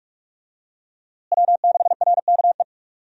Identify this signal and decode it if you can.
Morse code, heard as a keyed tone, about 42 wpm, 700 Hz. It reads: W6RKE
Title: USA Call Signs - 5 Characters - Repeated 3x - 45wpm